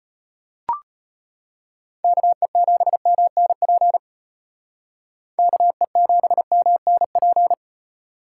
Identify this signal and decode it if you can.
Morse code, heard as a keyed tone, about 31 wpm, 700 Hz. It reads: KE7MNP KE7MNP
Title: Speed Racing - USA Call Signs 25wpm